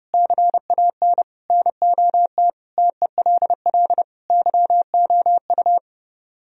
Morse code, heard as a keyed tone, about 37 wpm, 700 Hz. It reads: CAN NOT TELL YOU
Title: Speed-Racing - Sets of 5 Words 30wpm